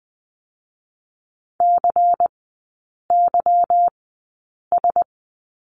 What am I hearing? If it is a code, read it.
Morse code, heard as a keyed tone, about 20 wpm, spaced as for 8 wpm, 690 Hz. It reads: CYS
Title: 145 Lesson 24e - Letter y - Three Characters 20wpm